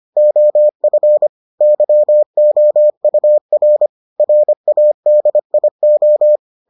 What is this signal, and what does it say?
Morse code, heard as a keyed tone, about 25 wpm, 600 Hz. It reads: OF YOUR RADIO